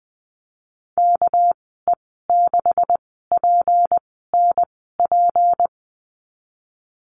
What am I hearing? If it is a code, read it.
Morse code, heard as a keyed tone, about 20 wpm, spaced as for 15 wpm, 700 Hz. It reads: KE6PNP
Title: 200 Lesson 32g - Letter k - Call Signs 20wpm